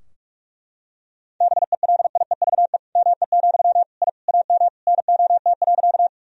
Morse code, heard as a keyed tone, about 42 wpm, 700 Hz. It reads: BELIEVE ME, I AM NOT.
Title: Sentences from Top 1000 Words - Review Part 2 - 45wpm